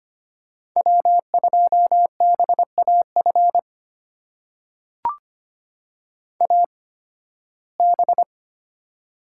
Morse code, heard as a keyed tone, about 25 wpm, spaced as for 16 wpm, 700 Hz. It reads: W2BAF AB
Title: USA Call Signs - 5 Characters - 8x Character Spacing - 25wpm